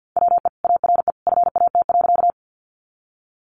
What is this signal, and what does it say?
Morse code, heard as a keyed tone, about 50 wpm, 700 Hz. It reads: WE ARE FAT.
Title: Sentences from Top 600 Words 50wpm